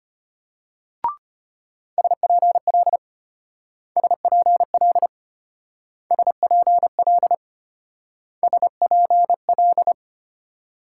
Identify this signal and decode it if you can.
Morse code, heard as a keyed tone, about 32 wpm, 700 Hz. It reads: SPL SPL SPL SPL